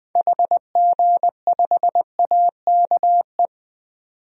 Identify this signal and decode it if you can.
Morse code, heard as a keyed tone, about 21 wpm, 700 Hz. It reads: HG5AKE
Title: DXCC-100 Call Signs - 6 Characters - 20wpm